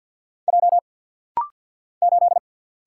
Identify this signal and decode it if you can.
Morse code, heard as a keyed tone, about 50 wpm, 700 Hz. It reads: J 8
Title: Single Letter-Number - Rapid-Fire 50wpm